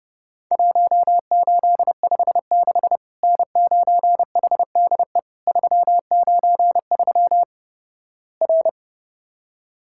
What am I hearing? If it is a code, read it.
Morse code, heard as a keyed tone, about 32 wpm, 700 Hz. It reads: 1856 N9HDE 393 R
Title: Worked All Europe DX Contest - QTC Exchange - 30wpm